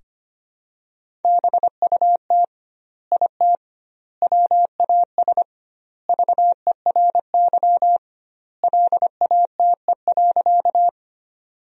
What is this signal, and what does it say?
Morse code, heard as a keyed tone, about 25 wpm, 700 Hz. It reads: BUT IT WAS VERY LATE.
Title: 2x Word-Spacing - Sentences from Top 300 Words - 25wpm